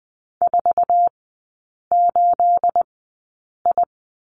Morse code, heard as a keyed tone, about 20 wpm, spaced as for 8 wpm, 690 Hz. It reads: 48I
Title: 212 Lesson 34e - Number 8 - Three Characters 20wpm